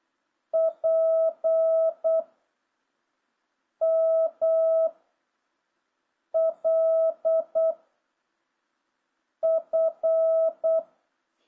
Morse code, heard as a keyed tone, about 8 wpm, 660 Hz. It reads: P M L F